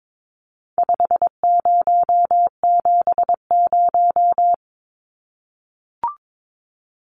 Morse code, heard as a keyed tone, about 22 wpm, 700 Hz. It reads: 5070
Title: Numbers - 4-Digit 22wpm